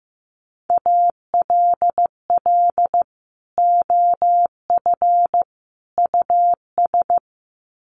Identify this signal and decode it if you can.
Morse code, heard as a keyed tone, about 15 wpm, 700 Hz. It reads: ALL OF US
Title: Sets of 3 Words 15wpm